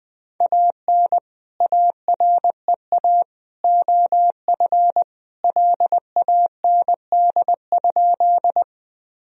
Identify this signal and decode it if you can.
Morse code, heard as a keyed tone, about 20 wpm, 700 Hz. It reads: AN AREA OF LAND?
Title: Sentences from Top 1000 Words - Review Part 2 - 20wpm